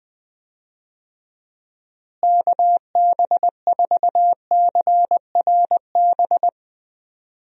Morse code, heard as a keyed tone, about 21 wpm, 700 Hz. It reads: KB4CRB
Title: USA Call Signs - 6 Characters - Repeated 2x - 20wpm